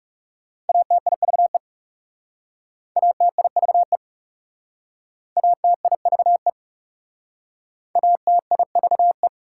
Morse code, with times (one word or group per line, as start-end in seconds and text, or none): 0.7-1.6 ATIVE
3.0-4.0 ATIVE
5.4-6.5 ATIVE
8.0-9.3 ATIVE